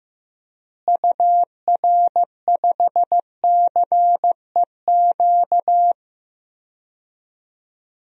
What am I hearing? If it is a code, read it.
Morse code, heard as a keyed tone, about 15 wpm, 700 Hz. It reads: UR5CEQ
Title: DXCC-100 Call Signs - 6 Characters - Repeated 2x - 15wpm